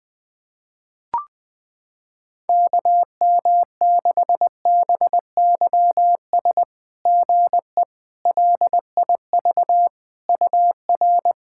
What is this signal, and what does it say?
Morse code, heard as a keyed tone, about 21 wpm, 690 Hz. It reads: KM6BYS GE LIV UR
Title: SOTA QSO Practice 20wpm